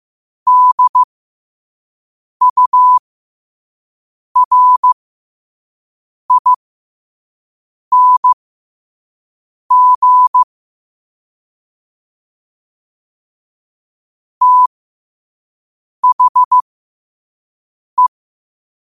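Morse code, heard as a keyed tone, about 15 wpm, spaced as for 5 wpm, 1000 Hz. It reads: DURING THE
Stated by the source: 5 WPM Morse quote of the day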